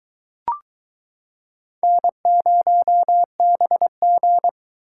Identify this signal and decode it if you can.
Morse code, heard as a keyed tone, about 22 wpm, 700 Hz. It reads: N0BG